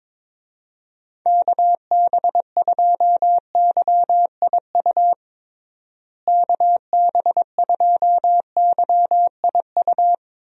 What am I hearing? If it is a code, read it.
Morse code, heard as a keyed tone, about 22 wpm, 700 Hz. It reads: KB2YIU KB2YIU
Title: USA Call Signs - 6 Characters - Repeated 3x - 22wpm